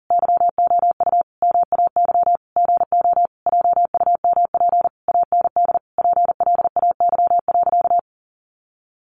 Morse code, heard as a keyed tone, about 40 wpm, 700 Hz. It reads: YOU MAY GO JUMP AND PLAY.
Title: Sentences from Top 1300 Words 40wpm